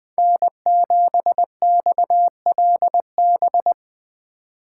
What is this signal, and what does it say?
Morse code, heard as a keyed tone, about 20 wpm, 700 Hz. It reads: N7XLB
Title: USA Call Signs - 5 Characters - 3x Character Spacing - 20wpm